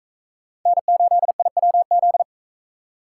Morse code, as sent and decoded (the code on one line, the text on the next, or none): -. ---.. .. .-- --..
N8IWZ